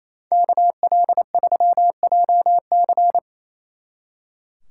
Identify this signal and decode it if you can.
Morse code, heard as a keyed tone, about 35 wpm, 700 Hz. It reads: KL3JC
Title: Speed Racing - USA 2x2 Call Signs 28wpm